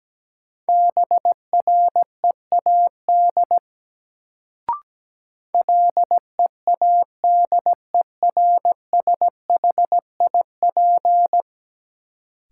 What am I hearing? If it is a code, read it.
Morse code, heard as a keyed tone, about 17 wpm, 700 Hz. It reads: BREAD LEADERSHIP
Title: Sentences from Top 1600 Words - New Word Drill Alt - 17wpm